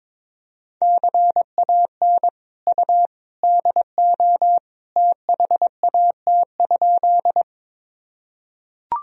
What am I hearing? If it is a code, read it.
Morse code, heard as a keyed tone, about 21 wpm, 700 Hz. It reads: CAN U DO THAT?
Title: Sentences from Top 100 Words and Abbreviations 22wpm